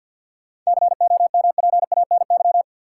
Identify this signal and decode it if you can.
Morse code, heard as a keyed tone, about 52 wpm, 700 Hz. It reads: COMPANY
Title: Sentences from Top 600 Words - New Word Drill - 50wpm